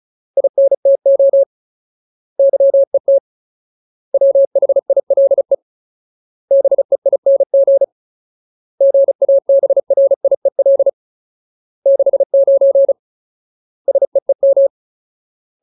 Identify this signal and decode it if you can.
Morse code, heard as a keyed tone, about 34 wpm, 550 Hz. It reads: INTO YET WHILE BEING GABRIEL 69 SEEM
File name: Ultimate Word List / 35WPM Ultimate